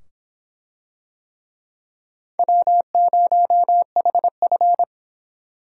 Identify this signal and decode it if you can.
Morse code, heard as a keyed tone, about 24 wpm, 700 Hz. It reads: W0HF